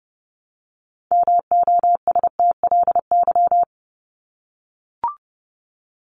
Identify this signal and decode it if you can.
Morse code, heard as a keyed tone, about 30 wpm, 700 Hz. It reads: MOSTLY
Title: Top 1001-1250 Words 30wpm